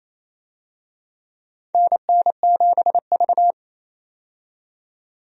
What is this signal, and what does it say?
Morse code, heard as a keyed tone, about 28 wpm, 700 Hz. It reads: NN7V
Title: USA Call Signs - 4 Characters - 28wpm